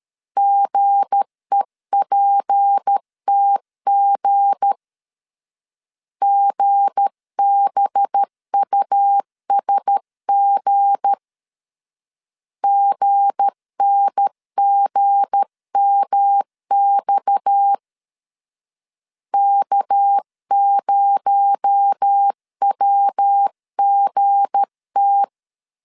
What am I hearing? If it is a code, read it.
Morse code, heard as a keyed tone, about 13 wpm, 800 Hz. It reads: GEPTG GBUSG GNGMX K0WGT
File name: LEZIONI / 7 GF9Z / 7 G 12wpm 2055